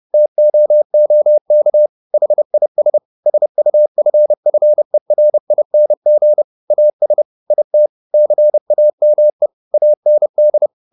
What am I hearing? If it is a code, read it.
Morse code, heard as a keyed tone, about 30 wpm, 600 Hz. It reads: TOOK HIS SUFFERING AS IT CAME AND
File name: the old man and the sea / the old man and the sea 1 30wpm